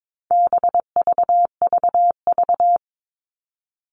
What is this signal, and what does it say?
Morse code, heard as a keyed tone, about 22 wpm, 700 Hz. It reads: BVVV
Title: All Cut Numbers - 4-Digit 22wpm